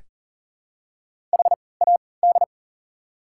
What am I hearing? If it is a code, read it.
Morse code, heard as a keyed tone, about 40 wpm, spaced as for 23 wpm, 690 Hz. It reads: HAD
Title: 3x Character Spacing - Top 500 Words - 40wpm